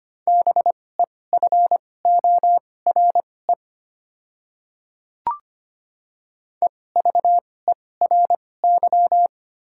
Morse code, heard as a keyed tone, about 25 wpm, spaced as for 18 wpm, 700 Hz. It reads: BEFORE EVERY
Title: 2x Character Spacing - Top 200 Words - 25wpm